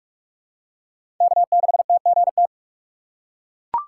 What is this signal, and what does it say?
Morse code, heard as a keyed tone, about 43 wpm, 700 Hz. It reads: K6TGT
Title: USA Call Signs - 5 Characters - 6x Character Spacing - 45wpm